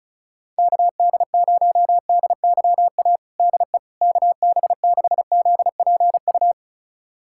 Morse code, heard as a keyed tone, about 34 wpm, 700 Hz. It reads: KD0DYA DE KB6ZPU
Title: USA Callsign DE Callsign 35wpm